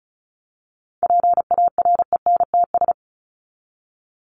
Morse code, heard as a keyed tone, about 34 wpm, 700 Hz. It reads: PARENTS
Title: ES:S Suffix - Top 100 Words with Base Words 35wpm